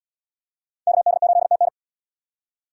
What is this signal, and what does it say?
Morse code, heard as a keyed tone, about 70 wpm, 700 Hz. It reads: KD8TN